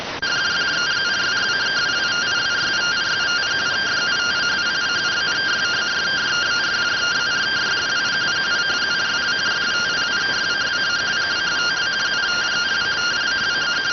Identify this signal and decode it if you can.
Two-tone teleprinter signal, about 170 Hz shift.